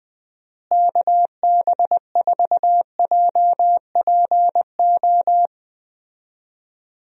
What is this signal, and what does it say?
Morse code, heard as a keyed tone, about 20 wpm, 700 Hz. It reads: KB4JPO
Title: USA Call Signs - 6 Characters - Repeated 3x - 20wpm